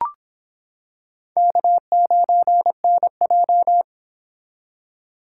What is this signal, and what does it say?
Morse code, heard as a keyed tone, about 25 wpm, 700 Hz. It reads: K9NJ